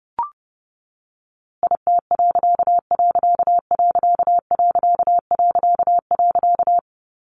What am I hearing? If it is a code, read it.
Morse code, heard as a keyed tone, about 28 wpm, 700 Hz. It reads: IT......
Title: A Sound Of Thunder 30wpm